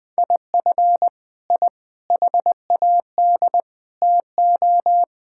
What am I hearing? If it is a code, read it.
Morse code, heard as a keyed tone, about 20 wpm, 700 Hz. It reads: IF I HAD TO
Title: Sets of 9 Words - 20wpm